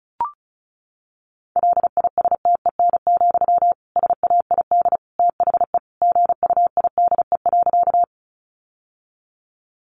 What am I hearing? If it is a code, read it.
Morse code, heard as a keyed tone, about 34 wpm, 700 Hz. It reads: LISTEN, SAID THE GUIDE.
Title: Sentences from Top 600 Words 35wpm